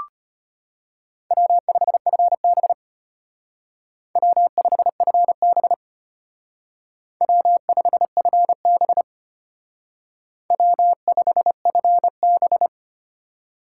Morse code, heard as a keyed tone, about 32 wpm, 700 Hz. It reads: W5FB W5FB W5FB W5FB